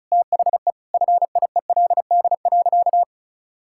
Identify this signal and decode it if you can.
Morse code, heard as a keyed tone, about 34 wpm, 700 Hz. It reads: THE FIELD.
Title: Sentences from Top 1000 Words - Review Part 5 - 35wpm